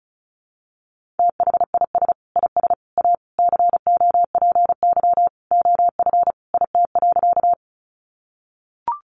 Morse code, heard as a keyed tone, about 34 wpm, 700 Hz. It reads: THIS IS A COPY OF IT.
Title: Sentences from Top 1400 Words 35wpm